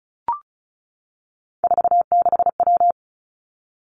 Morse code, heard as a keyed tone, about 36 wpm, 700 Hz. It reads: 46W
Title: Three Challenging Letter-Number Combinations 35wpm